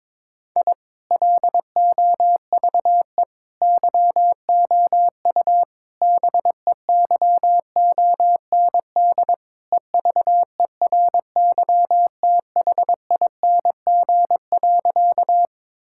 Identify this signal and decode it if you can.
Morse code, heard as a keyed tone, about 21 wpm, 700 Hz. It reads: I LOVE YOU BEYOND EVERYTHING.
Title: Sentences from Top 1000 Words - Review Part 6 - 22wpm